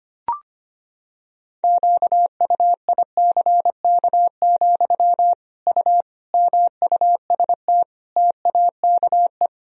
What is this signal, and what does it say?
Morse code, heard as a keyed tone, about 25 wpm, 700 Hz. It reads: QUICK, U MUST TAKE
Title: Sentences from Top 500 Words and Abbreviations 25wpm